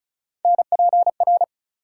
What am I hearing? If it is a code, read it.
Morse code, heard as a keyed tone, about 36 wpm, 700 Hz. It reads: NPR
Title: Sets of 5 Words - New Word Drill - 35wpm